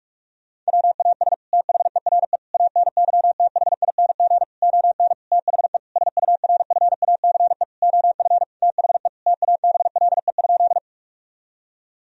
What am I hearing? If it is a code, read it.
Morse code, heard as a keyed tone, about 41 wpm, 700 Hz. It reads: WAS THERE ANYTHING ON THE SURFACE OF THE TABLE?
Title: Sentences from Top 1000 Words - Review Part 3 - 45wpm